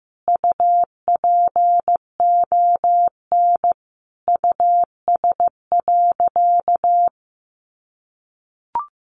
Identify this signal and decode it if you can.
Morse code, heard as a keyed tone, about 15 wpm, 700 Hz. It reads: UPON US.
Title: Sentences from Top 1500 Words 15wpm